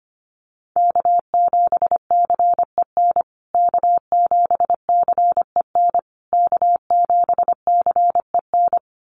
Morse code, heard as a keyed tone, about 25 wpm, 700 Hz. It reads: K7CEN K7CEN K7CEN
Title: USA Call Signs Repeated 3x 25wpm